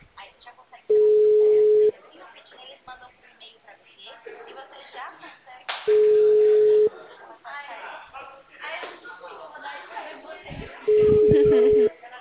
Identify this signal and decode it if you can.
Ringback tone (Europe, 425 Hz), with speech.